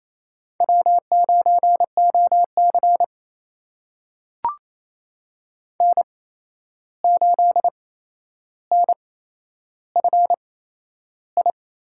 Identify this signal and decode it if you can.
Morse code, heard as a keyed tone, about 28 wpm, spaced as for 11 wpm, 700 Hz. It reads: W9OC N8NFI
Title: USA Call Signs - 8x Character Spacing - 28wpm